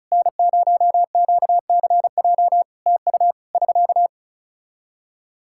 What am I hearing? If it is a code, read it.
Morse code, heard as a keyed tone, about 34 wpm, 700 Hz. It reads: N0QCJ TU <SK>
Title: Practice QSO - Exchange 4 - Ending - 35wpm